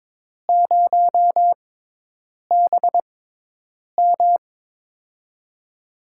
Morse code, heard as a keyed tone, about 21 wpm, spaced as for 8 wpm, 700 Hz. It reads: 0BM
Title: USA Call Signs - 4 Characters - 6x Character Spacing - 22wpm